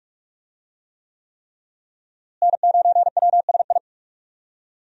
Morse code, heard as a keyed tone, about 46 wpm, 700 Hz. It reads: N9WSI
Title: USA Call Signs - 5 Characters - Repeated 3x - 45wpm